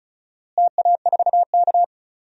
Morse code, heard as a keyed tone, about 34 wpm, 700 Hz. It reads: TA4K
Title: DXCC-100 Call Signs - 4 Characters - Repeated 2x - 35wpm